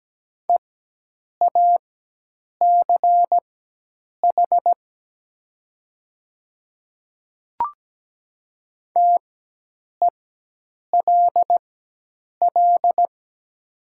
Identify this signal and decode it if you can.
Morse code, heard as a keyed tone, about 17 wpm, spaced as for 8 wpm, 700 Hz. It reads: EACH TELL
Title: 4x Character Spacing - Top 300 Words - 17wpm